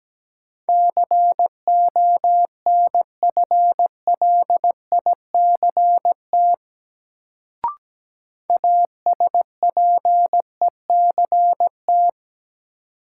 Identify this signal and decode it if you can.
Morse code, heard as a keyed tone, about 17 wpm, 700 Hz. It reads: CONFLICT ASPECT
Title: Sentences from Top 1300 Words - New Word Drill Alt - 17wpm